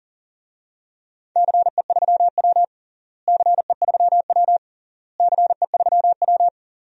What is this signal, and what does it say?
Morse code, heard as a keyed tone, about 40 wpm, 700 Hz. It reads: CE3W CE3W CE3W
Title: DXCC-100 Call Signs - 4 Characters - Repeated 3x - 40wpm